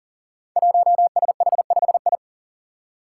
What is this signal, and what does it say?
Morse code, heard as a keyed tone, about 42 wpm, 700 Hz. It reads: 1SH5I
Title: Five Challenging Letter-Number Combinations 40wpm